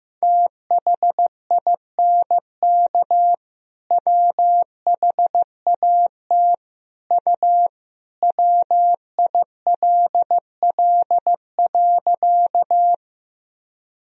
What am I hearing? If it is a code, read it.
Morse code, heard as a keyed tone, about 15 wpm, 700 Hz. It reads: THINK WHAT U WILL.
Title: Sentences from Top 200 Words and Abbreviations 15wpm